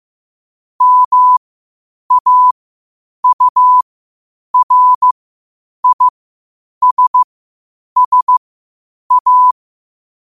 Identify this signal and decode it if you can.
Morse code, heard as a keyed tone, about 15 wpm, spaced as for 8 wpm, 1000 Hz. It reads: MAURISSA
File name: qotdfiles / 8wpm